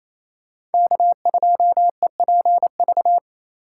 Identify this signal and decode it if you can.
Morse code, heard as a keyed tone, about 28 wpm, 700 Hz. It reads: K2EPV
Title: USA Call Signs - 5 Characters - 2x Character Spacing - 28wpm